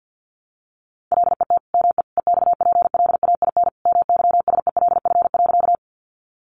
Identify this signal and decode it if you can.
Morse code, heard as a keyed tone, about 49 wpm, 700 Hz. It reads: LET ME EXPLAIN MYSELF.
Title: Sentences from Top 1300 Words 50wpm